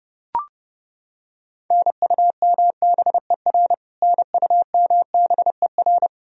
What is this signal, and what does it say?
Morse code, heard as a keyed tone, about 30 wpm, 700 Hz. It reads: NUMBER NUMBER
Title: Top 100 Words Repeated 30wpm